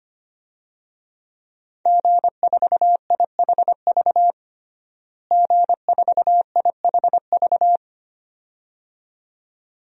Morse code, heard as a keyed tone, about 25 wpm, 700 Hz. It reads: G4IHV G4IHV
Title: DXCC-100 Call Signs - 5 Characters - Repeated 2x - 25wpm